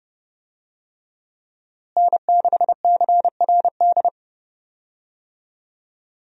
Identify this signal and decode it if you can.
Morse code, heard as a keyed tone, about 30 wpm, 700 Hz. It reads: N6CRD
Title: USA Call Signs - 5 Characters - Repeated 3x - 30wpm